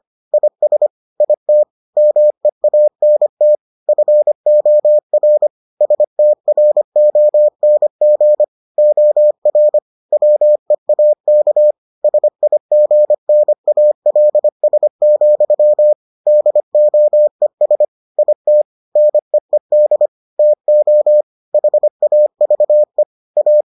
Morse code, heard as a keyed tone, about 25 wpm, 600 Hz. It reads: IS IT MEANT FOR STRONG OR WEAK SIGNALS, DOES IT NEED TO HAVE A